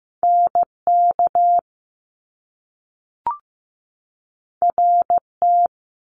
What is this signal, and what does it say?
Morse code, heard as a keyed tone, about 15 wpm, 700 Hz. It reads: NK RT
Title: Top Two Letter Combinations in Words 15wpm